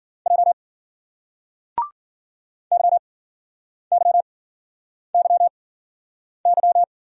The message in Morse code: J Y Y Y Y

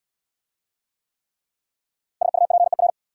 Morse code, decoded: SUPER

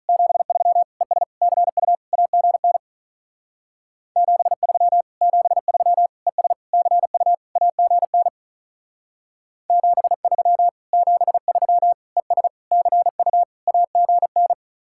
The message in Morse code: 73 ES CU AGN 73 73 ES CU AGN 73 73 ES CU AGN

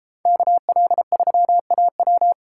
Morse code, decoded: KL3AW

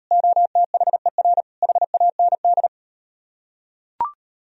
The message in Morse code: OTHER HAND

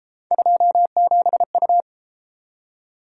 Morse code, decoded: 27U

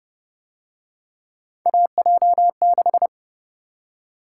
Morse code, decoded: AJ6

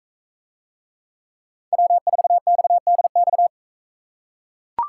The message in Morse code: W4XDX

